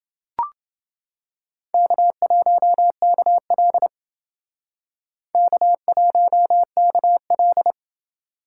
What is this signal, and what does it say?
Morse code, heard as a keyed tone, about 28 wpm, 700 Hz. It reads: K1KL K1KL